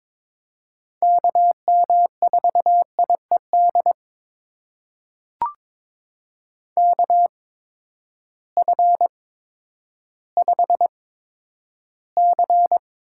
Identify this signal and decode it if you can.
Morse code, heard as a keyed tone, about 21 wpm, spaced as for 11 wpm, 700 Hz. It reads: KM4IED KF5C
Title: USA Call Signs - 6 Characters - 8x Character Spacing - 22wpm